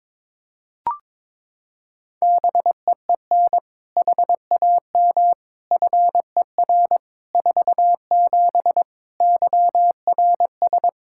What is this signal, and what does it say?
Morse code, heard as a keyed tone, about 22 wpm, 700 Hz. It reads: BEEN HAM FER 47 YRS
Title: Practice QSO - Exchange 3 - More Info - 22wpm